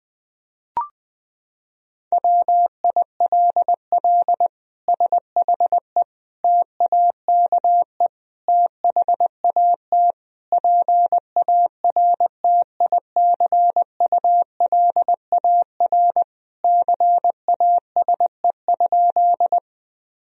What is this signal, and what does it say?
Morse code, heard as a keyed tone, about 20 wpm, 700 Hz. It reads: WILL SHE TAKE THAT PARTICULAR CASE?
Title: Sentences from Top 1000 Words - Review Part 4 - 20wpm